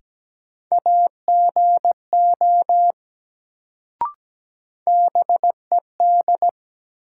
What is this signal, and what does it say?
Morse code, heard as a keyed tone, about 17 wpm, 700 Hz. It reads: AGO BED
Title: Sentences from Top 500 Words - New Word Drill - 17wpm